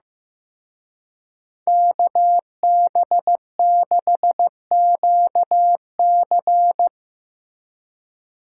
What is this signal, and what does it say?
Morse code, heard as a keyed tone, about 16 wpm, 700 Hz. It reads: KB6QC
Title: Speed Racing - USA 2x2 Call Signs 15wpm